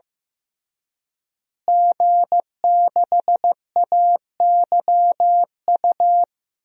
Morse code, heard as a keyed tone, about 15 wpm, 700 Hz. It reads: G6AYU
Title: DXCC-100 Call Signs - 5 Characters - Repeated 3x - 15wpm